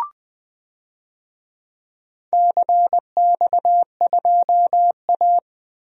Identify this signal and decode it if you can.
Morse code, heard as a keyed tone, about 21 wpm, 700 Hz. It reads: CX2A
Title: DXCC-100 Call Signs - 4 Characters - Repeated 3x - 20wpm